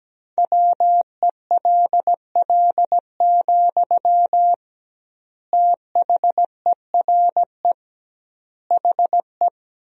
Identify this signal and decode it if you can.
Morse code, heard as a keyed tone, about 17 wpm, 700 Hz. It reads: WELL, THERE HE
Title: 2x Word-Spacing - Sentences from Top 500 Words - 17wpm